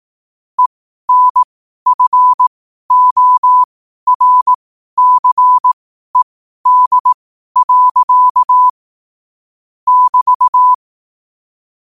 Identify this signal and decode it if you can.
Morse code, heard as a keyed tone, about 18 wpm, spaced as for 12 wpm, 1000 Hz. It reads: ENFORCED. <BT>
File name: qotdfiles / 12wpm